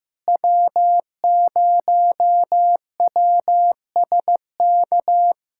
Morse code, heard as a keyed tone, about 15 wpm, 700 Hz. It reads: W0WSK
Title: SOTA QSO Practice 15wpm